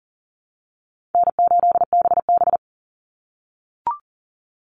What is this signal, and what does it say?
Morse code, heard as a keyed tone, about 40 wpm, 700 Hz. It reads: N8BB
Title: NA SOTA Chaser Callsigns 40wpm